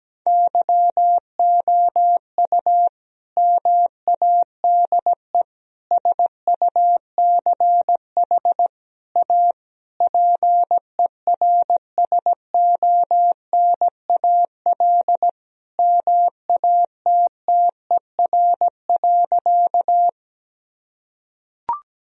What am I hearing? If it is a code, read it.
Morse code, heard as a keyed tone, about 17 wpm, 700 Hz. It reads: YOU MADE SUCH A PERSONAL MATTER.
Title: Sentences from Top 1000 Words - Review Part 4 - 17wpm